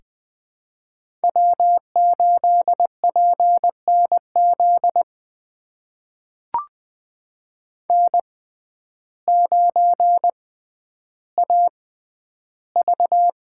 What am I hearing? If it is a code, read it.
Morse code, heard as a keyed tone, about 20 wpm, spaced as for 11 wpm, 700 Hz. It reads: W8PNZ N9AV
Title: USA Call Signs - 5 Characters - 6x Character Spacing - 20wpm